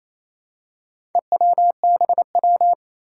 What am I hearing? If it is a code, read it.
Morse code, heard as a keyed tone, about 28 wpm, 700 Hz. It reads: EWBW
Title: Four Challenging Letter-Number Combinations 28wpm